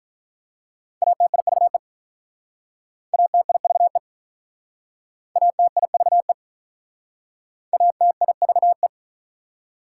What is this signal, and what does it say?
Morse code, heard as a keyed tone, about 46 wpm, 700 Hz. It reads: ATIVE ATIVE ATIVE ATIVE